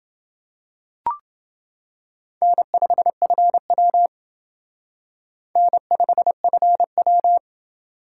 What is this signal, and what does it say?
Morse code, heard as a keyed tone, about 28 wpm, 700 Hz. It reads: N5FW N5FW